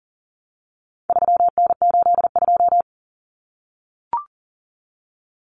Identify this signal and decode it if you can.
Morse code, heard as a keyed tone, about 40 wpm, 700 Hz. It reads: 3N82
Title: Common Cut Numbers - 4-Digit 40wpm